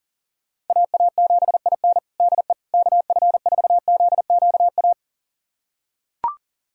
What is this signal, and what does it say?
Morse code, heard as a keyed tone, about 37 wpm, 700 Hz. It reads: AA7IN DE KF4ZQA
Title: USA Callsign DE Callsign 40wpm